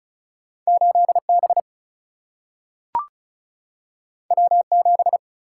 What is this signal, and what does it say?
Morse code, heard as a keyed tone, about 36 wpm, 700 Hz. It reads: 8B W7
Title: Single-Letter Single-Number 35wpm